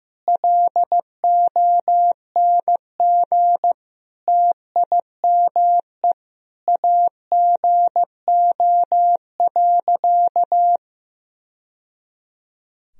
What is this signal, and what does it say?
Morse code, heard as a keyed tone, about 15 wpm, 700 Hz. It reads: LONG TIME AGO.
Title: Sentences from Top 1000 Words - Review Part 3 - 15wpm